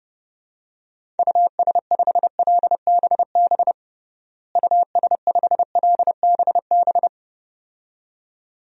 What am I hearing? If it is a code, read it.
Morse code, heard as a keyed tone, about 31 wpm, 700 Hz. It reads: US5LBB US5LBB
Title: DXCC-100 Call Signs - 6 Characters - Repeated 2x - 30wpm